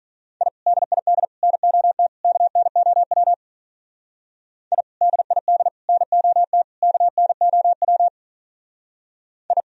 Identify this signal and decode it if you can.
Morse code, heard as a keyed tone, about 42 wpm, 700 Hz. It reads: I DID NOT KNOW I DID NOT KNOW I
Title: Speed-Racing - Sets of 4 Words 35wpm